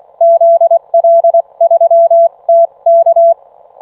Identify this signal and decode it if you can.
Morse code, heard as a keyed tone, about 24 wpm, 670 Hz. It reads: ZL3TK